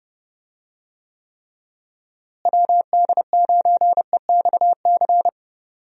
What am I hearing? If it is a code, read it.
Morse code, heard as a keyed tone, about 31 wpm, 700 Hz. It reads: WD9EXC
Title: USA Call Signs - 6 Characters - Repeated 3x - 30wpm